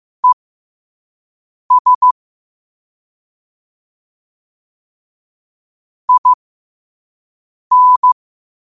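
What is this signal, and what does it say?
Morse code, heard as a keyed tone, about 15 wpm, spaced as for 5 wpm, 1000 Hz. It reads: ES IN
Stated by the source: Morse code sent at 5 wpm.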